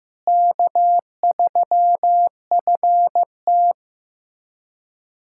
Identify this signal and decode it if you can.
Morse code, heard as a keyed tone, about 15 wpm, 700 Hz. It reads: K3FT
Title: USA Callsign DE Callsign - 4 Characters - 15wpm